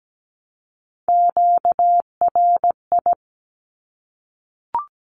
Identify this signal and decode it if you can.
Morse code, heard as a keyed tone, about 17 wpm, 700 Hz. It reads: QRI
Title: Q-Codes 17wpm